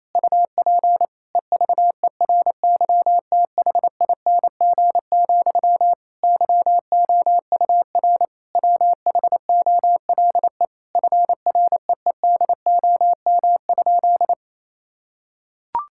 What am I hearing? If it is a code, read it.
Morse code, heard as a keyed tone, about 28 wpm, 700 Hz. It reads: UP EVERYTHING, YOUR WHOLE FREEDOM?
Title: Sentences from Top 1000 Words - Review Part 5 - 28wpm